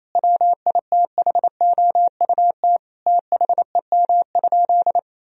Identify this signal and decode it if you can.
Morse code, heard as a keyed tone, about 28 wpm, 700 Hz. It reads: WITHOUT THEM?
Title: Sentences from Top 1000 Words - Review Part 3 - 28wpm